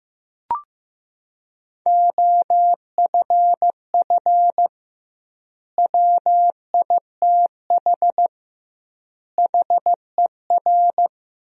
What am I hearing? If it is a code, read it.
Morse code, heard as a keyed tone, about 16 wpm, 700 Hz. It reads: OFF WITH HER
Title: 2x Word-Spacing - Sentences from Top 300 Words - 15wpm